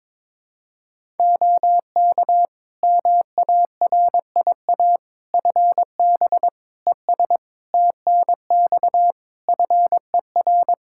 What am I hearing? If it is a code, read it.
Morse code, heard as a keyed tone, about 22 wpm, 700 Hz. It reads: OK MARIA FB ES TNX FER
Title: Practice QSO - Exchange 2 - Info - 22wpm